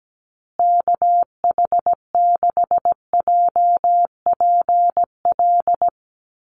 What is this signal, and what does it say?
Morse code, heard as a keyed tone, about 17 wpm, 700 Hz. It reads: KH6JPL
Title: USA Call Signs Repeated 3x 17wpm